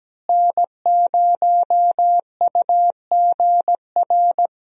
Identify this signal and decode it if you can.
Morse code, heard as a keyed tone, about 17 wpm, 700 Hz. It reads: N0UGR
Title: USA Call Signs - 5 Characters - Repeated 3x - 17wpm